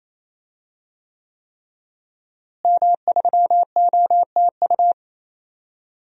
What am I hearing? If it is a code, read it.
Morse code, heard as a keyed tone, about 28 wpm, 700 Hz. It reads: M3OTU
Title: DXCC-100 Call Signs - 5 Characters - Repeated 3x - 28wpm